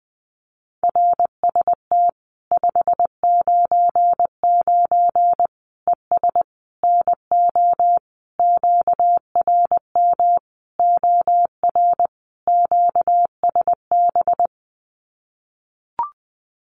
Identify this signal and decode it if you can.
Morse code, heard as a keyed tone, about 20 wpm, 700 Hz. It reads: RST 599 ES NO QRM OR QSB
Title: Common QSO Phrases 20wpm